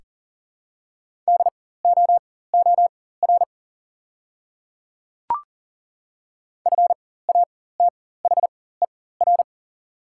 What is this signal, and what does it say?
Morse code, heard as a keyed tone, about 40 wpm, spaced as for 19 wpm, 690 Hz. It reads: DOOR FATHER
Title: 4x Character Spacing - Top 500 Words - 40wpm